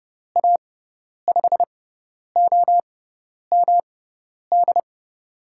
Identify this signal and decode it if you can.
Morse code, heard as a keyed tone, about 30 wpm, spaced as for 10 wpm, 700 Hz. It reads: A5OMD
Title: USA Call Signs - 6x Character Spacing - 30wpm